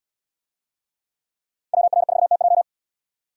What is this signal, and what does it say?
Morse code, heard as a keyed tone, about 72 wpm, 700 Hz. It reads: WR4TJ